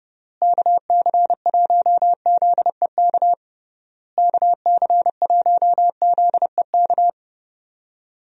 Morse code, heard as a keyed tone, about 32 wpm, 700 Hz. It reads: KC1ZEK KC1ZEK
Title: USA Call Signs - 6 Characters - Repeated 3x - 30wpm